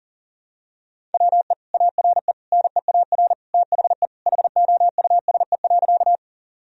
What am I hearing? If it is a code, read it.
Morse code, heard as a keyed tone, about 40 wpm, 700 Hz. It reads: WE ARE NEAR THE HOUSE.
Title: Sentences from Top 1000 Words - Review Part 1 - 40wpm